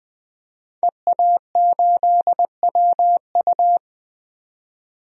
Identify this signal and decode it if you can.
Morse code, heard as a keyed tone, about 20 wpm, 700 Hz. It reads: EA8WU
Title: DXCC-100 Call Signs - 5 Characters - 20wpm